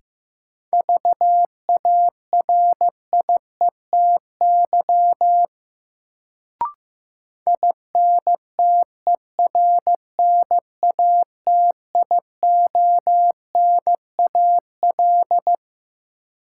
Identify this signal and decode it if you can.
Morse code, heard as a keyed tone, about 15 wpm, 700 Hz. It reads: VARIETY INTERNATIONAL
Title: Sentences from Top 1000 Words - Review Part 4 - New Word Drill - 15wpm